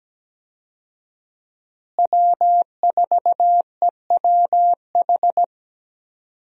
Morse code, heard as a keyed tone, about 17 wpm, 700 Hz. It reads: W4EWH
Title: USA Call Signs - 5 Characters - Repeated 3x - 17wpm